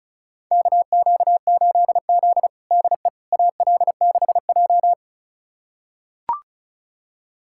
Morse code, heard as a keyed tone, about 34 wpm, 700 Hz. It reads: KQ8Z DE AL6J
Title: USA Callsign DE Callsign - 4 Characters - 35wpm